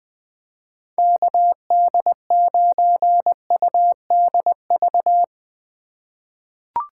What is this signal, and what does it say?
Morse code, heard as a keyed tone, about 20 wpm, 700 Hz. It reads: KD9UDV
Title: 2023 POTA Activator Call Signs - 20wpm